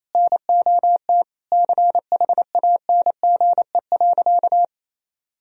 Morse code, heard as a keyed tone, about 28 wpm, 700 Hz. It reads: NOT CHANGE.
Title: Sentences from Top 1000 Words - Review Part 3 - 28wpm